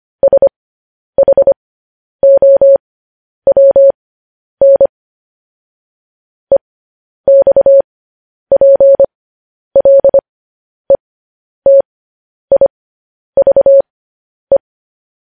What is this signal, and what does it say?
Morse code, heard as a keyed tone, about 25 wpm, spaced as for 10 wpm, 550 Hz. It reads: SHOWN EXPLETIVE
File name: podcast / News headlines-1x-25x10